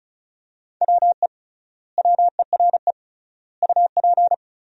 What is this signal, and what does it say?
Morse code, heard as a keyed tone, about 36 wpm, 700 Hz. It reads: WE WERE UP
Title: 3x Word-Spacing - Sentences from Top 400 Words - 35wpm